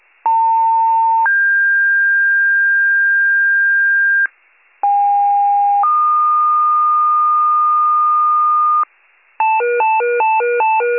Two-tone page, heard as a one-second tone, then a three-second tone, about 900 then 1600 Hz, about 810 then 1220 Hz, with noise. It is heard from 0.2 to 8.8 s.